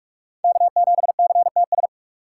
Morse code, heard as a keyed tone, about 47 wpm, 700 Hz. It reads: K7CTS
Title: USA Call Signs - 5 Characters - 4x Character Spacing - 45wpm